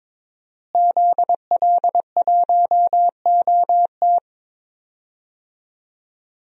Morse code, heard as a keyed tone, about 22 wpm, 700 Hz. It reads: ZL1OT
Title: DXCC-100 Call Signs - 5 Characters - Repeated 2x - 22wpm